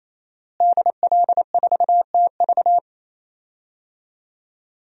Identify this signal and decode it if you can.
Morse code, heard as a keyed tone, about 28 wpm, 700 Hz. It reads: DL4TV
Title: DXCC-100 Call Signs - 5 Characters - Repeated 3x - 28wpm